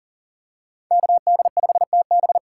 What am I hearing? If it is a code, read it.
Morse code, heard as a keyed tone, about 39 wpm, 700 Hz. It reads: KD5TB
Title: USA Call Signs - 5 Characters - 3x Character Spacing - 40wpm